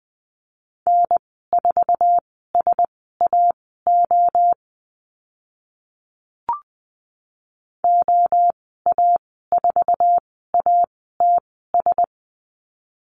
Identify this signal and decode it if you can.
Morse code, heard as a keyed tone, about 20 wpm, spaced as for 15 wpm, 700 Hz. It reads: N4SAO OA4ATS
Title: 062 Lesson 11g - Letter h - Call Signs 20wpm